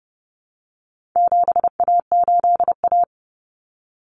Morse code, heard as a keyed tone, about 30 wpm, 700 Hz. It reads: 7A8A
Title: Common Cut Numbers - 4-Digit 30wpm